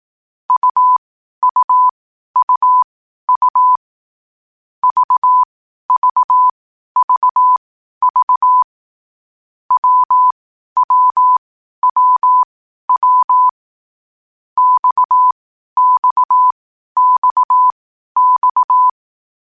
Morse code, heard as a keyed tone, about 18 wpm, spaced as for 12 wpm, 1000 Hz. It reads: UUUU VVVV WWWW XXXX
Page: Hier findet Ihr das Morsealphabet in langsamer Geschwindigkeit zum Hören. Dabei ist jeder Buchstabe vier Mal hintereinander aufgenommen worden, danach kommt direkt der nächste.